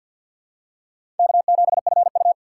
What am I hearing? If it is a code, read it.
Morse code, heard as a keyed tone, about 51 wpm, 700 Hz. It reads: K7FU
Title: USA Call Signs - 4 Characters - 2x Character Spacing - 50wpm